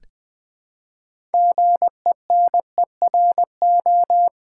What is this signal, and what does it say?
Morse code, heard as a keyed tone, about 20 wpm, 700 Hz. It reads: GENERO
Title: Build A Word - 3 Letters At A Time - 8-Letter Words - 20wpm